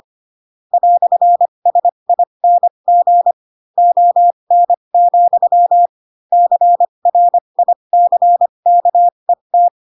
Morse code, heard as a keyed tone, about 25 wpm, spaced as for 22 wpm, 700 Hz. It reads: "SING ON, CRICKET
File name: Pinocchio / Pinocchio ch4 18 25 ik1hge